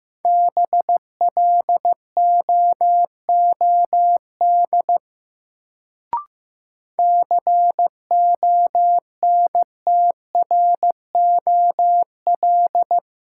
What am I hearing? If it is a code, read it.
Morse code, heard as a keyed tone, about 15 wpm, 700 Hz. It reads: BLOOD CONTROL
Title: Sentences from Top 1000 Words - Review Part 2 - New Word Drill - 15wpm